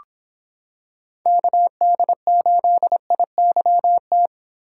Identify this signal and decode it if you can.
Morse code, heard as a keyed tone, about 26 wpm, 700 Hz. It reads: KD8IYT